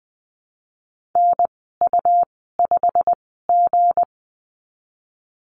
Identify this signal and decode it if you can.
Morse code, heard as a keyed tone, about 19 wpm, spaced as for 14 wpm, 700 Hz. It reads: NU5G
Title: 167 Lesson 27g - Number 7 - Call Signs 20wpm